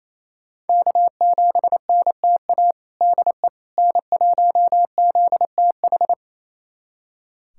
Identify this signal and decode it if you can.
Morse code, heard as a keyed tone, about 28 wpm, 700 Hz. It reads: K7NTA DE N1ZTH
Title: USA Callsign DE Callsign - 5 Characters - 28wpm